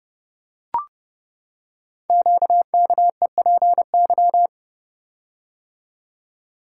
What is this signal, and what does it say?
Morse code, heard as a keyed tone, about 30 wpm, 700 Hz. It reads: QKEPY
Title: Code Groups - 5 Letters - 30wpm